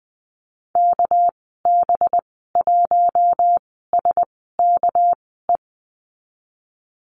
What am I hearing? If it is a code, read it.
Morse code, heard as a keyed tone, about 19 wpm, 700 Hz. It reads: K B 1 S K E
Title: 207 Lesson 33g - Letter j - Call Signs 20wpm